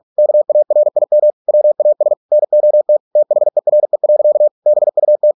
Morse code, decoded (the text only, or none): KARIM WAS NOT THERE. BUT